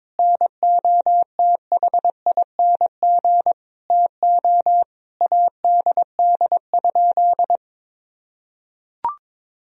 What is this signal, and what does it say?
Morse code, heard as a keyed tone, about 21 wpm, 700 Hz. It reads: NOTHING TO ADD?
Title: Sentences from Top 1000 Words - Review Part 1 - 22wpm